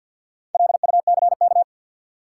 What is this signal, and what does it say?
Morse code, heard as a keyed tone, about 53 wpm, 700 Hz. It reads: LUCK
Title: Sentences from Top 1500 Words - New Word Drill Alt - 50wpm